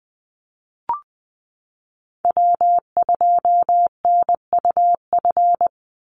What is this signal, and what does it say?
Morse code, heard as a keyed tone, about 20 wpm, 700 Hz. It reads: W2NUF
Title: USA Call Signs 20wpm